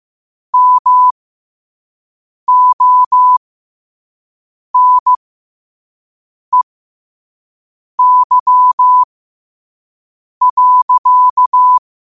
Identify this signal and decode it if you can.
Morse code, heard as a keyed tone, about 15 wpm, spaced as for 5 wpm, 1000 Hz. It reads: MONEY.